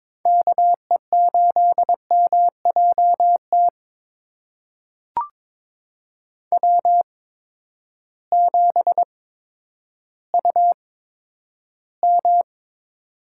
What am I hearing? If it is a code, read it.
Morse code, heard as a keyed tone, about 23 wpm, spaced as for 11 wpm, 700 Hz. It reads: KE8MJT W7UM
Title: USA Call Signs - 8x Character Spacing - 22wpm